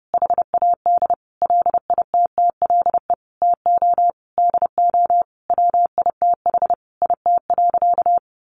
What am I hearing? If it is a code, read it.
Morse code, heard as a keyed tone, about 30 wpm, 700 Hz. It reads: HAD LITTLE TO DO WITH IT.
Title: Sentences from Top 1500 Words 30wpm